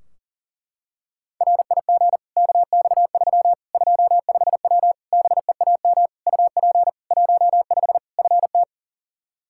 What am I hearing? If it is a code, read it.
Morse code, heard as a keyed tone, about 38 wpm, 700 Hz. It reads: RIG KX3 25W BEAM UP 15 FT
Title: K1USN SST QSO Practice 40wpm